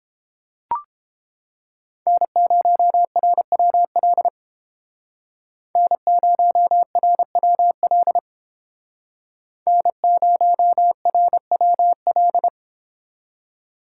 Morse code, heard as a keyed tone, about 30 wpm, 700 Hz. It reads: N0RWL N0RWL N0RWL